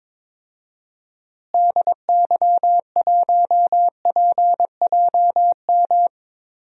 Morse code, heard as a keyed tone, about 22 wpm, 700 Hz. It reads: DY1PJM
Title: DXCC-100 Call Signs - 6 Characters - Repeated 3x - 22wpm